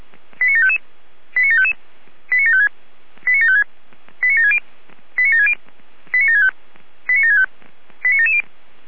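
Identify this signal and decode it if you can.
Five-tone selective call (ZVEI), heard as quick runs of five tones, tones of about 70 ms. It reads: 89755 89755 89756 89756 89760 89760 89765 89765 89770
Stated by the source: ZVEI five-tone selective call received on SDR.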